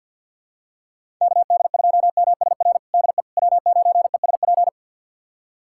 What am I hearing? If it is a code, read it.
Morse code, heard as a keyed tone, about 47 wpm, 700 Hz. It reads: KD2MSR DE W9ESP
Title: USA Callsign DE Callsign 50wpm